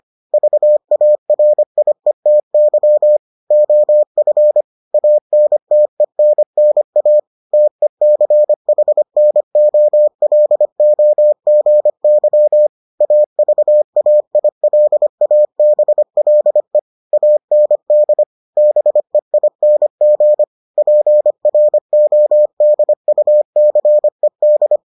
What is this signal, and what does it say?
Morse code, heard as a keyed tone, about 25 wpm, 600 Hz. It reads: VARIETY OF ANTENNA TECHNOLOGY AVAILABLE AND BEING PRODUCED